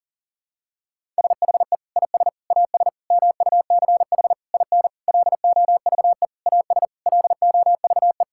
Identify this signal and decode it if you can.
Morse code, heard as a keyed tone, about 40 wpm, 700 Hz. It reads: SHE IS AS MUCH IN LOVE AS LOVE
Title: Sentences from Top 1000 Words - Review Part 5 - 40wpm